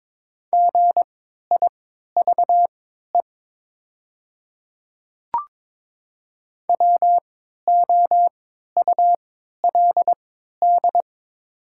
Morse code, heard as a keyed tone, about 22 wpm, spaced as for 13 wpm, 700 Hz. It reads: GIVE WOULD
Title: 3x Character Spacing - Top 200 Words - 22wpm